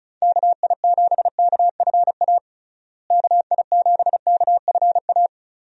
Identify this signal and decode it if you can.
Morse code, heard as a keyed tone, about 36 wpm, 700 Hz. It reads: KI7KFA KI7KFA
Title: USA Call Signs - 6 Characters - Repeated 3x - 35wpm